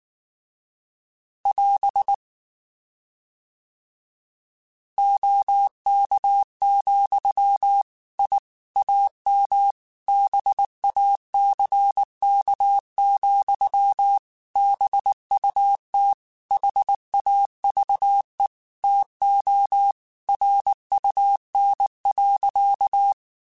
Morse code, heard as a keyed tone, about 19 wpm, 800 Hz. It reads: <AS> OK, I AM BACK, BUT HAVE TO RUN.
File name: QSOs / QSOs-mp3s / WA7QOX 2@18